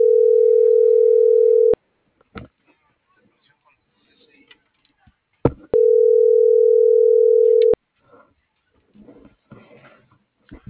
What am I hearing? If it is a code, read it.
Ringback tone (North America), 440 + 480 Hz.